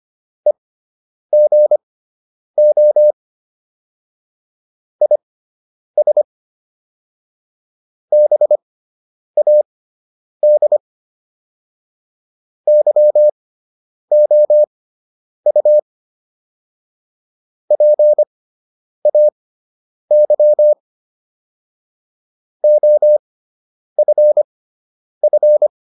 Morse code, heard as a keyed tone, about 25 wpm, spaced as for 9 wpm, 600 Hz. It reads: EGO IS BAD YOU PAY OFF